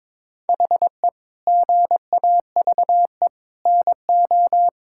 Morse code, heard as a keyed tone, about 22 wpm, 700 Hz. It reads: HE GAVE NO
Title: Sentences from Top 1600 Words 22wpm